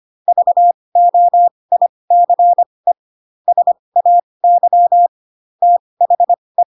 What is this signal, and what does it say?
Morse code, heard as a keyed tone, about 25 wpm, spaced as for 20 wpm, 700 Hz. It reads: VOICE SAY THE
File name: Pinocchio / Pinocchio ch1 15 25 ik1hge